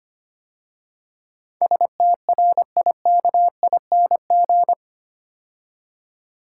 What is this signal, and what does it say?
Morse code, heard as a keyed tone, about 25 wpm, 700 Hz. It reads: STRIKING
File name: 8 Letter Words - First 3 Letters Given - 25wpm